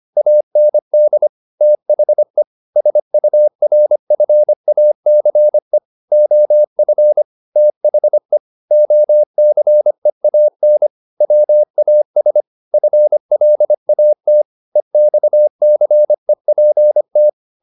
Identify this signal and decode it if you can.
Morse code, heard as a keyed tone, about 25 wpm, 600 Hz. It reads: AND THE SURFACE OF THE OCEAN WAS FLAT EXCEPT
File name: the old man and the sea part 1 25wpm